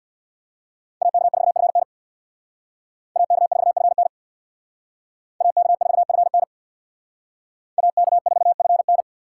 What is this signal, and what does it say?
Morse code, heard as a keyed tone, about 62 wpm, 700 Hz. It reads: AK4FN AK4FN AK4FN AK4FN
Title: Speed Racing - USA 2x2 Call Signs 50wpm